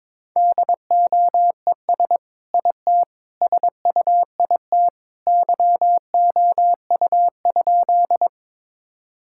Morse code, heard as a keyed tone, about 22 wpm, 700 Hz. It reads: DOES IT SUIT YOU?
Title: Sentences from Top 1600 Words 22wpm